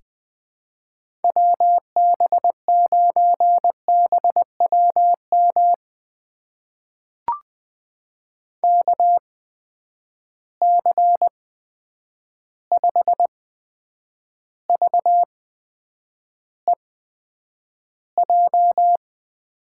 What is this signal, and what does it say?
Morse code, heard as a keyed tone, about 20 wpm, spaced as for 9 wpm, 700 Hz. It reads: WB9BWM KC5VEJ
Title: USA Call Signs - 6 Characters - 8x Character Spacing - 20wpm